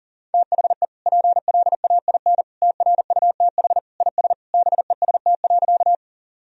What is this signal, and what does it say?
Morse code, heard as a keyed tone, about 40 wpm, 700 Hz. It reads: THE PLAIN TRUTH IS BEST.
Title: Sentences from Top 1000 Words - Review Part 2 - 40wpm